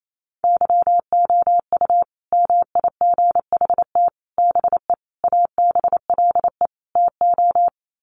Morse code, heard as a keyed tone, about 28 wpm, 700 Hz. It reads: YOU MIGHT BE ABLE TO
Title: Sets of 5 Words Encore2 28wpm